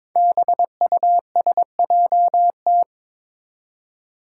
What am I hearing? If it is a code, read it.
Morse code, heard as a keyed tone, about 22 wpm, 700 Hz. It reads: BUSJT
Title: Five Challenging Letter-Number Combinations 22wpm